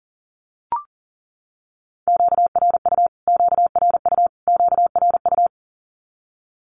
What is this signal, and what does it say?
Morse code, heard as a keyed tone, about 40 wpm, 700 Hz. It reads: QRU QRU QRU
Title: Q-Codes Repeated 3x 40wpm